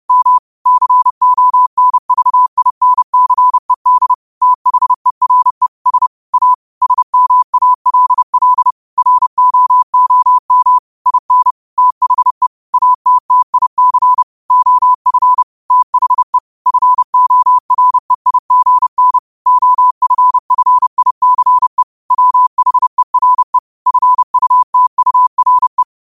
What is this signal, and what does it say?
Morse code, heard as a keyed tone, about 30 wpm, 1000 Hz. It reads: M CONVINCED THERE S A SMALL ROOM IN THE ATTIC OF THE FOREIGN OFFICE WHERE FUTURE
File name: 30wpm